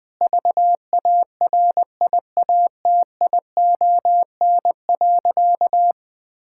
Morse code, heard as a keyed tone, about 20 wpm, 700 Hz. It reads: VARIATION.
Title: Sentences from Top 1600 Words 20wpm